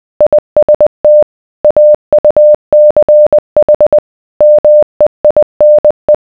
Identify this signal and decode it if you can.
Morse code, heard as a keyed tone, about 20 wpm, 600 Hz. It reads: IST AUCH MEINE